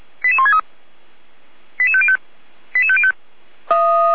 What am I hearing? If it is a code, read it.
Five-tone selective call (ZVEI), heard as quick runs of five tones, tones of about 70 ms. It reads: 88163 88505 88505